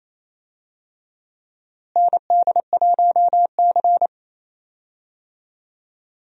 Morse code, heard as a keyed tone, about 28 wpm, 700 Hz. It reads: ND1C
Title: USA Call Signs - 4 Characters - Repeated 3x - 28wpm